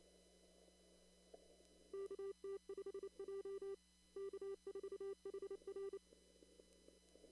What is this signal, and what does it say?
Morse code, heard as a keyed tone, about 26 wpm, 390 Hz. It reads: KT5J K4HR
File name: CW / kt5j wpx cw-op-k5pi